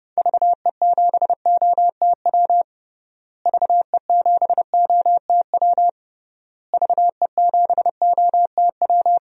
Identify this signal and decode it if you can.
Morse code, heard as a keyed tone, about 32 wpm, 700 Hz. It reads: VE7OTW VE7OTW VE7OTW
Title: DXCC-100 Call Signs - 6 Characters - Repeated 3x - 30wpm